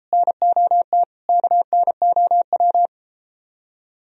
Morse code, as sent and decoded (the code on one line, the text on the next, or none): -. --- - / -.- -. --- .--
NOT KNOW